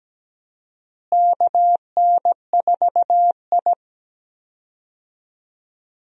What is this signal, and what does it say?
Morse code, heard as a keyed tone, about 17 wpm, 700 Hz. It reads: KN4I
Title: USA Call Signs - 4 Characters - Repeated 3x - 17wpm